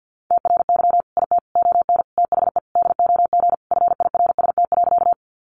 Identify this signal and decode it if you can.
Morse code, heard as a keyed tone, about 50 wpm, 700 Hz. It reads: TRY IT ON THE DOG FIRST.
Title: Sentences from Top 400 Words 50wpm